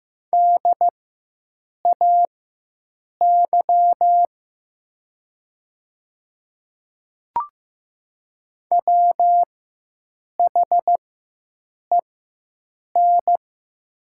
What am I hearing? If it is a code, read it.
Morse code, heard as a keyed tone, about 15 wpm, spaced as for 7 wpm, 700 Hz. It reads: DAY WHEN
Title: 4x Character Spacing - Top 100 Words - 15wpm